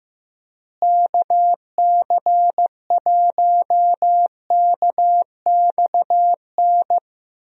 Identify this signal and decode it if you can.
Morse code, heard as a keyed tone, about 15 wpm, 700 Hz. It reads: KC1KXN
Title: USA Call Signs - 6 Characters - Repeated 2x - 15wpm